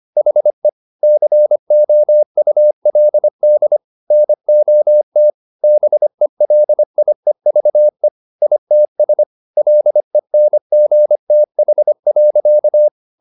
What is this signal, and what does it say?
Morse code, heard as a keyed tone, about 25 wpm, 600 Hz. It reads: HE COULD NOT BELIEVE ITS LENGTH.